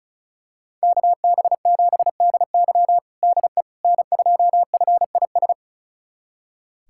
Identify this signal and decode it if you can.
Morse code, heard as a keyed tone, about 34 wpm, 700 Hz. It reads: KB7DY DE N2FIS
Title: USA Callsign DE Callsign - 5 Characters - 35wpm